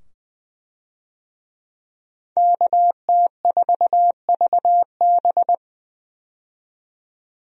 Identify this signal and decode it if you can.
Morse code, heard as a keyed tone, about 20 wpm, 700 Hz. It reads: KT4VB
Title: USA Call Signs - 5 Characters - 20wpm